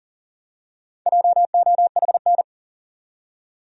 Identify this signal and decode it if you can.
Morse code, heard as a keyed tone, about 40 wpm, 700 Hz. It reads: JOHN
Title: USA Common Names - Max 5 Letters - 40wpm